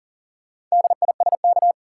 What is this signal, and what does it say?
Morse code, heard as a keyed tone, about 40 wpm, 700 Hz. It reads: DISK
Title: Sentences from Top 1000 Words - Review Part 6 - New Word Drill - 40wpm